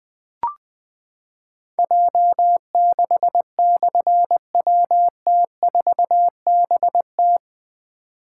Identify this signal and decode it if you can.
Morse code, heard as a keyed tone, about 20 wpm, 700 Hz. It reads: J6/WT4BT
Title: Difficult Call Signs 20wpm